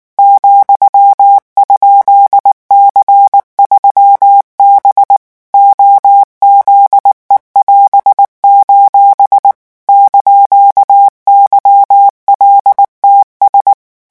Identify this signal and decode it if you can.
Morse code, heard as a keyed tone, about 19 wpm, 800 Hz. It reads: ,?C3B OZE<AS>: )YLTS
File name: morse mp3 / revision15-vitesse 18 mots